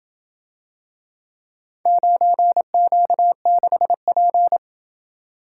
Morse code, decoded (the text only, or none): 9Q6P